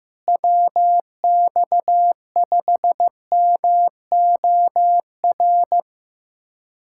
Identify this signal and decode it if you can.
Morse code, heard as a keyed tone, about 16 wpm, 700 Hz. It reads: WX5MOR
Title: USA Callsign DE Callsign - 6 Characters - 15wpm